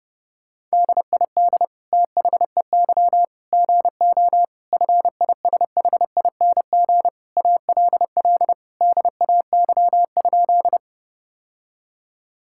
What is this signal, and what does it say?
Morse code, heard as a keyed tone, about 32 wpm, 700 Hz. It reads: DID THEY GO FISHING ALL DAY?
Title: Sentences from Top 1000 Words - Review Part 5 - 30wpm